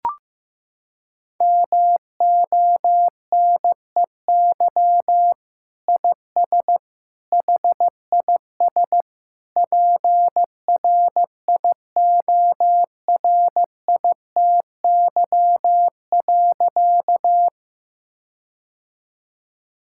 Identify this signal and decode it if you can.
Morse code, heard as a keyed tone, about 15 wpm, 700 Hz. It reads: MONEY IS HIS PRIORITY.
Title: Sentences from Top 1600 Words 15wpm